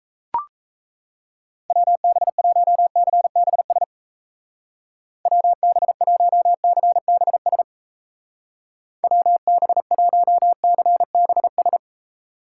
Morse code, heard as a keyed tone, about 33 wpm, 700 Hz. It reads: WB1CBS WB1CBS WB1CBS
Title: Speed Racing - USA Call Signs 28wpm